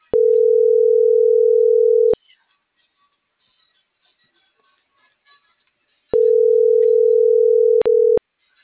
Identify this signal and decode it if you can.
Ringback tone (North America), 440 + 480 Hz.
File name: CENTRAL NMP / out melanie.diaznmp